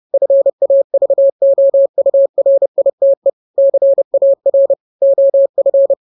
Morse code, heard as a keyed tone, about 30 wpm, 550 Hz. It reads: FAVOURITE CAR OF
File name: jokes / joke 13 30 0000